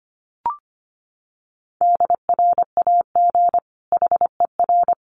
Morse code, heard as a keyed tone, about 25 wpm, 700 Hz. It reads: DRAG HER
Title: Sentences from Top 1500 Words 25wpm